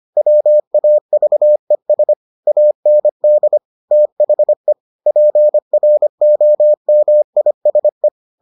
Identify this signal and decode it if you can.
Morse code, heard as a keyed tone, about 25 wpm, 600 Hz. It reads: WAVES AND THE PROMISE